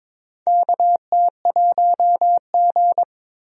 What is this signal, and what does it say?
Morse code, heard as a keyed tone, about 22 wpm, 700 Hz. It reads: KT1G
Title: USA Call Signs - 4 Characters - 22wpm